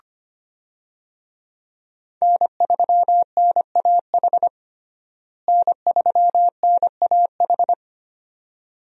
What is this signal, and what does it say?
Morse code, heard as a keyed tone, about 25 wpm, 700 Hz. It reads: N3NAH N3NAH
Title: USA Call Signs - 5 Characters - Repeated 2x - 25wpm